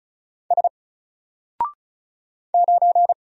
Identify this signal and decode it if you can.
Morse code, heard as a keyed tone, about 36 wpm, 700 Hz. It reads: S 9
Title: Single Character (Letter:Number:Punctuation:Prosign) Alt 35wpm